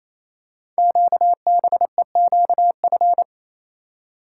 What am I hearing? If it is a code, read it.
Morse code, heard as a keyed tone, about 27 wpm, 700 Hz. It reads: QBEQF
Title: Code Groups - 5 Letters - 28wpm